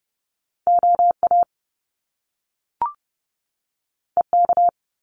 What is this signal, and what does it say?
Morse code, heard as a keyed tone, about 30 wpm, 700 Hz. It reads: OA EK
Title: Top Two Letter Combinations in Words 30wpm